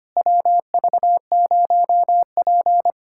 Morse code, heard as a keyed tone, about 25 wpm, 700 Hz. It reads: WV0P
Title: USA Callsign DE Callsign - 4 Characters - 25wpm